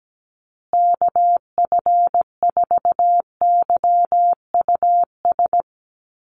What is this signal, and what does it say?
Morse code, heard as a keyed tone, about 17 wpm, 700 Hz. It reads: KF4YUS
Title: USA Call Signs Repeated 3x 17wpm